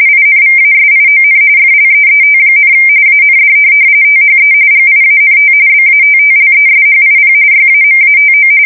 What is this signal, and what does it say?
RTTY, about 45.45 baud. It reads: INVASION FLEET, AND THE GIANT BATTLESHIP YAMATO
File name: fsk170 / USS-New-Jersey